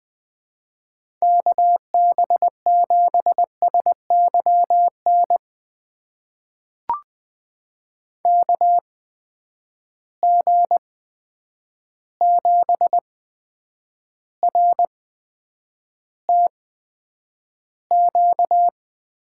Morse code, heard as a keyed tone, about 20 wpm, spaced as for 9 wpm, 700 Hz. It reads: KB7SYN KG7RTQ
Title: USA Call Signs - 6 Characters - 8x Character Spacing - 20wpm